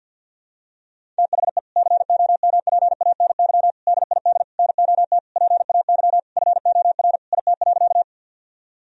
Morse code, heard as a keyed tone, about 50 wpm, 700 Hz. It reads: THE COMPANY DID NOT PAY FOR IT.
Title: Sentences from Top 1000 Words - Review Part 3 - 50wpm